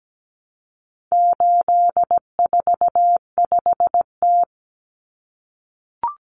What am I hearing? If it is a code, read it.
Morse code, heard as a keyed tone, about 17 wpm, 700 Hz. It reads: 845T
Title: Common Cut Numbers - 4-Digit 17wpm